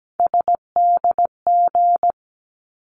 Morse code, heard as a keyed tone, about 17 wpm, 700 Hz. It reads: SDG
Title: ARRL Sections 17wpm